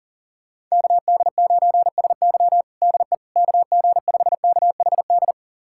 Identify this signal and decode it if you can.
Morse code, heard as a keyed tone, about 37 wpm, 700 Hz. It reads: KD9SY DE KG5KHD
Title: USA Callsign DE Callsign 40wpm